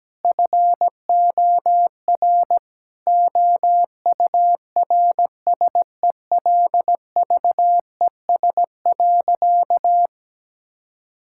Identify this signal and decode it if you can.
Morse code, heard as a keyed tone, about 17 wpm, 700 Hz. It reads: FOR OURSELVES.
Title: Sentences from Top 1600 Words 17wpm